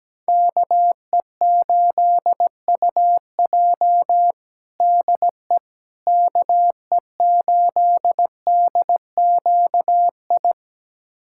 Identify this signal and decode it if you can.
Morse code, heard as a keyed tone, about 17 wpm, 700 Hz. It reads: KE8UJ DE KE8DQI
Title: USA Callsign DE Callsign 17wpm